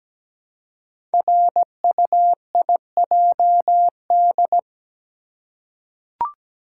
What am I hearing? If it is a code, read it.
Morse code, heard as a keyed tone, about 17 wpm, 700 Hz. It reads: RUIJD
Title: Code Groups - 5 Letters - 17wpm